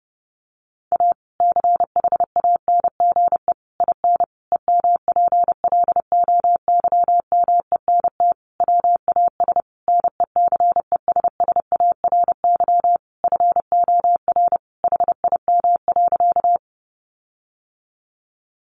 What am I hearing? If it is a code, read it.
Morse code, heard as a keyed tone, about 28 wpm, 700 Hz. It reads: A CHANGE IN EMPLOYMENT WAS NECESSARY FOR HIM.
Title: Sentences from Top 1500 Words 30wpm